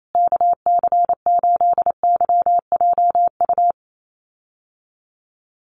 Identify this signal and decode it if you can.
Morse code, heard as a keyed tone, about 28 wpm, 700 Hz. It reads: KC8YJU
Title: USA Call Signs Repeated 3x 28wpm